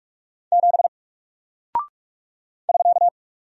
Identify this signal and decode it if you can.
Morse code, heard as a keyed tone, about 45 wpm, 700 Hz. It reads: 7 <SK>
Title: Single Character (Letter:Number:Punctuation:Prosign) Alt 45wpm